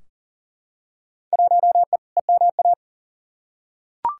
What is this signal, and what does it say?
Morse code, heard as a keyed tone, about 40 wpm, 700 Hz. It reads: 1E EMA
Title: Field Day Exchanges Encore - 40wpm